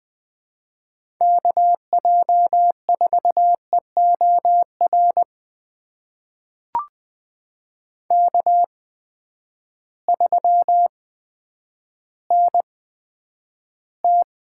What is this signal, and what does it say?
Morse code, heard as a keyed tone, about 20 wpm, spaced as for 10 wpm, 700 Hz. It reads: KJ4EOR K3NT
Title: USA Call Signs - 8x Character Spacing - 20wpm